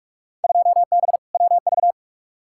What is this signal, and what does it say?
Morse code, heard as a keyed tone, about 47 wpm, 700 Hz. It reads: 2B WV
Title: Field Day Exchanges Encore - 45wpm